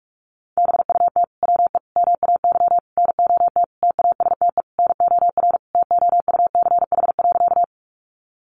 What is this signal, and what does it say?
Morse code, heard as a keyed tone, about 46 wpm, 700 Hz. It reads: BUT WE MAY NOT TASTE NOR TOUCH.
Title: Sentences from Top 1500 Words 45wpm